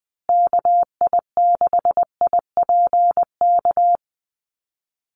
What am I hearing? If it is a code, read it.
Morse code, heard as a keyed tone, about 20 wpm, 700 Hz. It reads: KI6IPK
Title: USA Call Signs Repeated 2x 20wpm